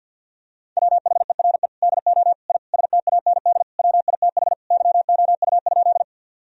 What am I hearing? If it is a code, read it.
Morse code, heard as a keyed tone, about 49 wpm, 700 Hz. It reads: WHERE DO I STAND WITH YOU?
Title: Sentences from Top 1000 Words - Review Part 1 - 50wpm